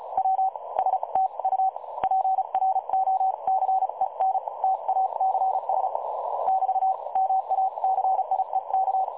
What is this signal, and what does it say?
Morse code, heard as a keyed tone, about 36 wpm, spaced as for 30 wpm, 760 Hz. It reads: W STU PROCENTACS <BT> MACIEK
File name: dlazaawansowanych / nagraniazpasma / websdr recording 3565.0kHz